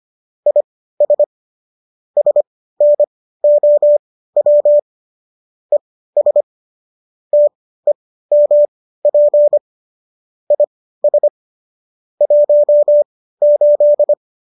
Morse code, heard as a keyed tone, about 25 wpm, spaced as for 15 wpm, 590 Hz. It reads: IS SNOW ES TEMP IS 18
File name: qso213 15